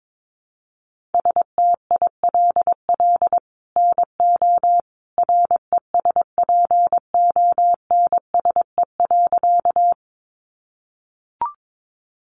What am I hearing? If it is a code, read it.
Morse code, heard as a keyed tone, about 22 wpm, 700 Hz. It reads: STILL NO RESPONSE.
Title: Sentences from Top 1400 Words 22wpm